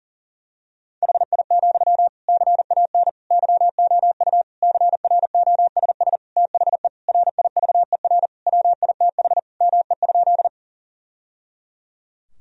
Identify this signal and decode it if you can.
Morse code, heard as a keyed tone, about 40 wpm, 700 Hz. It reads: HI, CAN YOU CROSS THE RIVER WITH ME?
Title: Sentences from Top 1000 Words - Review Part 3 - 40wpm